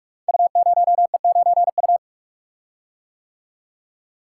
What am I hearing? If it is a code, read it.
Morse code, heard as a keyed tone, about 44 wpm, 700 Hz. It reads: U0E9U
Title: Code Groups - 5 Letter:Numbers - 45wpm